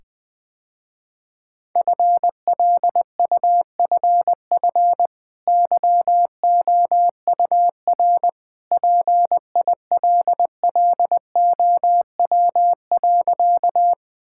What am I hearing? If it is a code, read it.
Morse code, heard as a keyed tone, about 21 wpm, 700 Hz. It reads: FLUFF YOUR PILLOW.
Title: Easy Sentences with Hints - 20wpm